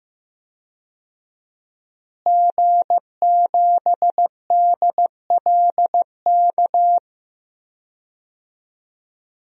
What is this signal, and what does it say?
Morse code, heard as a keyed tone, about 15 wpm, 700 Hz. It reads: G7DLK
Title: DXCC-100 Call Signs - 5 Characters - Repeated 2x - 15wpm